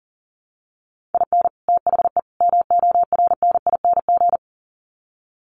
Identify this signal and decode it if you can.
Morse code, heard as a keyed tone, about 40 wpm, 700 Hz. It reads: IN THE MORNING
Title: Sets of 3 Words 40wpm